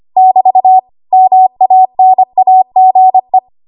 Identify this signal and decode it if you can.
Morse code, heard as a keyed tone, about 25 wpm, 750 Hz. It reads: <BT> MANAGE
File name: Archive / 25 WPM / 25WPM